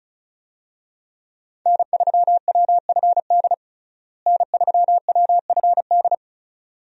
Morse code, heard as a keyed tone, about 34 wpm, 700 Hz. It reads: N3WFD N3WFD
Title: USA Call Signs - 5 Characters - Repeated 2x - 35wpm